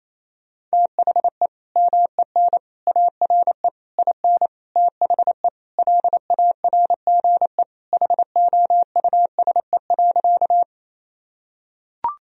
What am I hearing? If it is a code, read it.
Morse code, heard as a keyed tone, about 28 wpm, 700 Hz. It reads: THE MEN ARE IN THE LARGE HOUSE.
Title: Sentences from Top 1000 Words - Review Part 1 - 28wpm